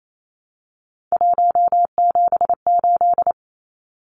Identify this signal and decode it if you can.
Morse code, heard as a keyed tone, about 28 wpm, 700 Hz. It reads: 178
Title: Numbers - 3-Digit 28wpm